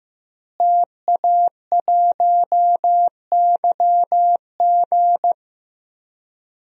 Morse code, heard as a keyed tone, about 15 wpm, 700 Hz. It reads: TA1YG
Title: DXCC-100 Call Signs - 5 Characters - Repeated 3x - 15wpm